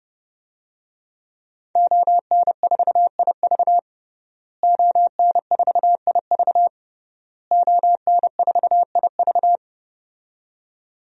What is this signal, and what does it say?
Morse code, heard as a keyed tone, about 32 wpm, 700 Hz. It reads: ON4IV ON4IV ON4IV
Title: DXCC-100 Call Signs - 5 Characters - Repeated 3x - 30wpm